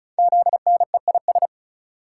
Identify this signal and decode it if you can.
Morse code, heard as a keyed tone, about 36 wpm, 700 Hz. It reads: ZNEIS
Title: Code Groups - 5 Letters - 35wpm